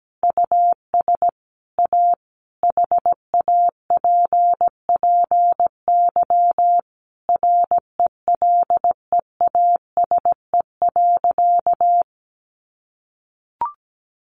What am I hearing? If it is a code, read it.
Morse code, heard as a keyed tone, about 17 wpm, 700 Hz. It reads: US A HAPPY RELEASE.
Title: Sentences from Top 1100 Words 17wpm